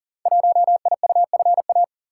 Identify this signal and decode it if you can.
Morse code, heard as a keyed tone, about 42 wpm, 700 Hz. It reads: 1IUFA
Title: Code Groups - 5 Letter:Numbers - 40wpm